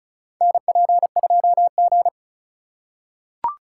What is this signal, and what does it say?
Morse code, heard as a keyed tone, about 36 wpm, 700 Hz. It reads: NP2G
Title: USA Call Signs - 4 Characters - 4x Character Spacing - 35wpm